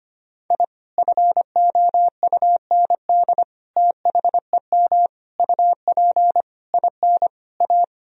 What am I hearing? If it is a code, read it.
Morse code, heard as a keyed tone, about 25 wpm, 700 Hz. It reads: I FOUND THEM UP IN A
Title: Sentences from Top 1000 Words - Review Part 1 - 25wpm